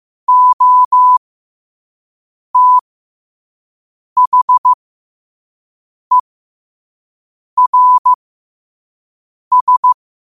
Morse code, heard as a keyed tone, about 15 wpm, spaced as for 5 wpm, 1000 Hz. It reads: OTHERS
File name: qotdfiles / 5wpm